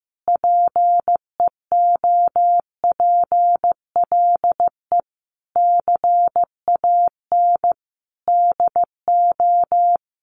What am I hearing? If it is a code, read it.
Morse code, heard as a keyed tone, about 15 wpm, 700 Hz. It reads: PEOPLE CAN DO
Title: Sentences from Top 1100 Words 15wpm